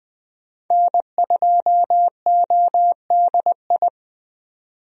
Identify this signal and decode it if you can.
Morse code, heard as a keyed tone, about 20 wpm, 700 Hz. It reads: N2ODI
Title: USA Call Signs - 5 Characters - 20wpm